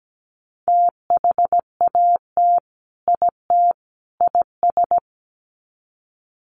Morse code, heard as a keyed tone, about 17 wpm, 700 Hz. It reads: THAT IT IS
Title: Sets of 3 Words 17wpm